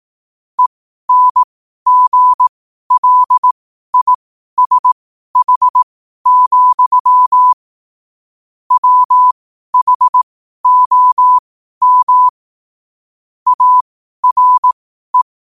Morse code, heard as a keyed tone, about 18 wpm, spaced as for 12 wpm, 1000 Hz. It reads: ENGLISH, WHOM ARE
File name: qotdfiles / 12wpm